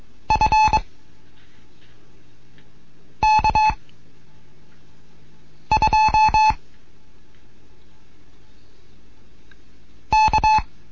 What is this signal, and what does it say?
Morse code, heard as a keyed tone, about 22 wpm, 860 Hz. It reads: F K 2 K